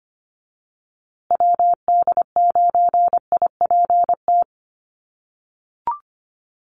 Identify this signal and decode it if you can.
Morse code, heard as a keyed tone, about 25 wpm, 700 Hz. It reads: WD9IPT
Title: USA Call Signs Repeated 3x 25wpm